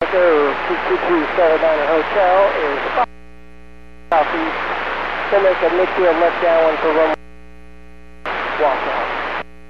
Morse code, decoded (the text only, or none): none